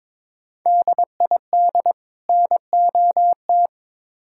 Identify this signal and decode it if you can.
Morse code, heard as a keyed tone, about 22 wpm, 700 Hz. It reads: DID NOT
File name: Build A Phrase - 2-Word Phrases - 22wpm